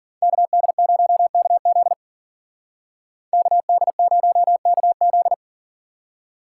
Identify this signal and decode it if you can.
Morse code, heard as a keyed tone, about 52 wpm, 700 Hz. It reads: KD0KZ KD0KZ